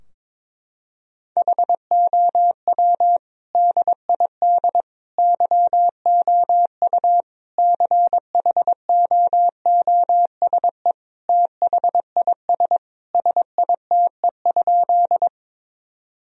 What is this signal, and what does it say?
Morse code, heard as a keyed tone, about 21 wpm, 700 Hz. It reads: HOW DID YOU CHOOSE THIS SITE?
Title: Sentences from Top 1000 Words - Review Part 4 - 22wpm